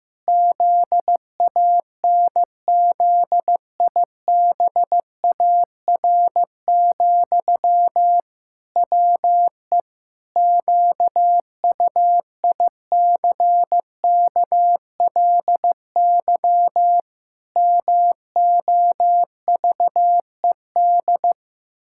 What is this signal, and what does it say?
Morse code, heard as a keyed tone, about 15 wpm, 700 Hz. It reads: ZANZIBAR, WE QUICKLY MOVED
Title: Pangram Challenge v2 - 15wpm